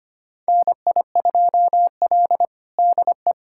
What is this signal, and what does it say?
Morse code, heard as a keyed tone, about 25 wpm, 700 Hz. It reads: NI2L DE
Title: USA Callsign DE Callsign 25wpm